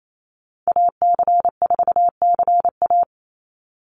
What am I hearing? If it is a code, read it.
Morse code, heard as a keyed tone, about 28 wpm, 700 Hz. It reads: AC4CA
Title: Call Signs Repeated 3x - 28wpm